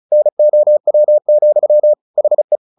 Morse code, heard as a keyed tone, about 34 wpm, 600 Hz. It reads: NOW, HE
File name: the old man and the sea 2 35wpm